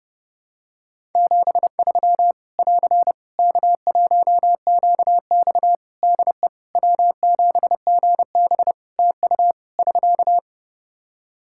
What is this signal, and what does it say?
Morse code, heard as a keyed tone, about 30 wpm, 700 Hz. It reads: 73 <AR> K1QX DE W7GB TU <SK>
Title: K1USN SST QSO Practice 30wpm